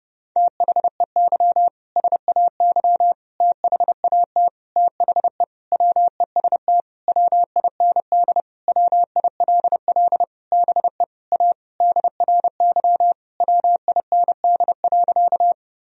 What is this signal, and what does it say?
Morse code, heard as a keyed tone, about 30 wpm, 700 Hz. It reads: THEY SAY THAT THE WEST WIND WILL BE A DRY WIND.
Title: Sentences from Top 1000 Words - Review Part 3 - 30wpm